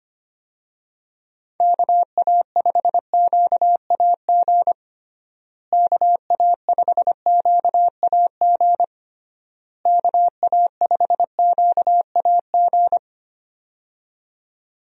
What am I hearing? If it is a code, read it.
Morse code, heard as a keyed tone, about 25 wpm, 700 Hz. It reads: KA5QAG KA5QAG KA5QAG
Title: USA Call Signs - 6 Characters - Repeated 3x - 25wpm